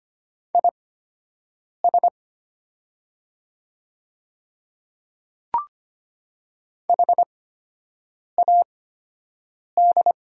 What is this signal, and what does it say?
Morse code, heard as a keyed tone, about 25 wpm, spaced as for 7 wpm, 700 Hz. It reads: IS HAD
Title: 8x Character Spacing - Top 300 Words - 25wpm